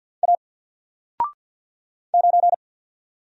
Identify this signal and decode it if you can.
Morse code, heard as a keyed tone, about 50 wpm, 700 Hz. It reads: A 9
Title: Single Letter-Number 50wpm